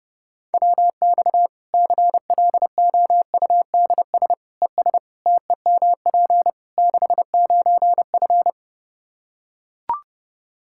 Morse code, heard as a keyed tone, about 31 wpm, 700 Hz. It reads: WX CLOUDS ES TEMP 69F
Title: K1USN SST QSO Practice 30wpm